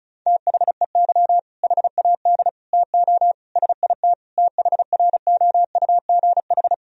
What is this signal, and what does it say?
Morse code, heard as a keyed tone, about 36 wpm, 700 Hz. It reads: THEY HAD TO SIT THROUGH
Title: Sentences from Top 1000 Words - Review Part 1 - 35wpm